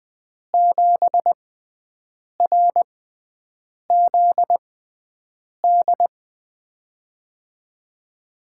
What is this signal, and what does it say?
Morse code, heard as a keyed tone, about 20 wpm, spaced as for 7 wpm, 700 Hz. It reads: 7RZD
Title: USA Call Signs - 6x Character Spacing - 20wpm